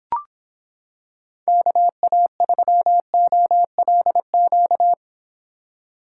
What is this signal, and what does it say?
Morse code, heard as a keyed tone, about 26 wpm, 700 Hz. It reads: KA3OLQ